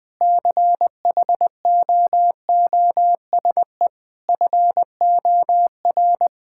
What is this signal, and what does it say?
Morse code, heard as a keyed tone, about 20 wpm, 700 Hz. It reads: CHOOSE FOR
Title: Difficult Sentences with Hints - 20wpm